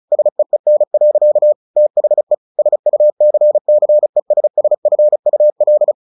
Morse code, heard as a keyed tone, about 34 wpm, 600 Hz. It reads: SEEN. THE SUCCESSFUL
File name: the old man and the sea / the old man and the sea 1 35wpm